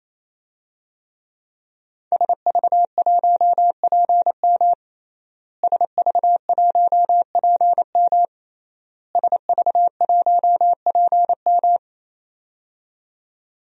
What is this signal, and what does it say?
Morse code, heard as a keyed tone, about 27 wpm, 700 Hz. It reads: SV1PM SV1PM SV1PM
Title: DXCC-100 Call Signs - 5 Characters - Repeated 3x - 28wpm